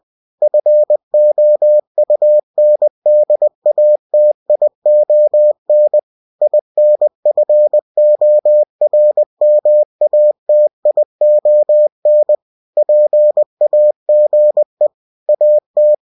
Morse code, heard as a keyed tone, about 20 wpm, 600 Hz. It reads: FOUNDATION INFORMATION PAGE AT
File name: cw-books / winnie-the-pooh / winnie-20wpm-page-0120